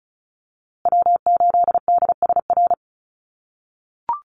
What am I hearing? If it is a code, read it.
Morse code, heard as a keyed tone, about 34 wpm, 700 Hz. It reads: W8DSR
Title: USA Call Signs Repeated 3x 35wpm